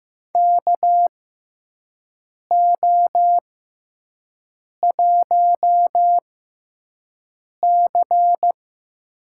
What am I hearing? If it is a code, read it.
Morse code, heard as a keyed tone, about 15 wpm, spaced as for 5 wpm, 700 Hz. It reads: KO1C
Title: USA Call Signs - 6x Character Spacing - 15wpm